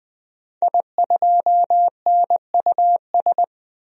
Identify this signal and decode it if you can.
Morse code, heard as a keyed tone, about 20 wpm, 700 Hz. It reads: I2NUS
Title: DXCC-100 Call Signs - 5 Characters - Repeated 2x - 20wpm